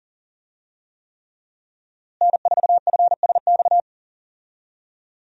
Morse code, heard as a keyed tone, about 40 wpm, 700 Hz. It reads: N4FSX
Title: USA Call Signs - 5 Characters - Repeated 3x - 40wpm